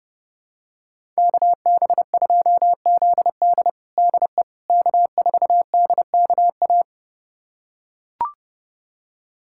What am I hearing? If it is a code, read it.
Morse code, heard as a keyed tone, about 32 wpm, 700 Hz. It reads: KB2ZD DE K4DKA
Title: USA Callsign DE Callsign - 5 Characters - 30wpm